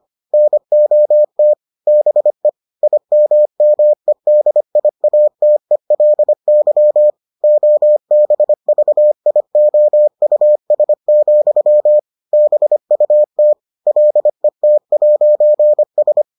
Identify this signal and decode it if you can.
Morse code, heard as a keyed tone, about 25 wpm, 600 Hz. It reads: NOT BE IMMEDIATELY OBVIOUS, BUT LET'S